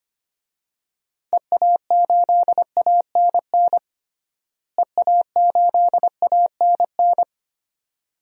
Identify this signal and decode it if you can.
Morse code, heard as a keyed tone, about 25 wpm, 700 Hz. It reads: EA8ANN EA8ANN
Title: DXCC-100 Call Signs - 6 Characters - Repeated 2x - 25wpm